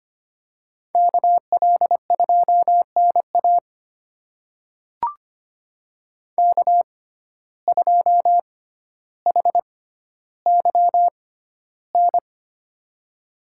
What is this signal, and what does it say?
Morse code, heard as a keyed tone, about 25 wpm, spaced as for 13 wpm, 700 Hz. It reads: KL2NA K2HYN
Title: USA Call Signs - 5 Characters - 6x Character Spacing - 25wpm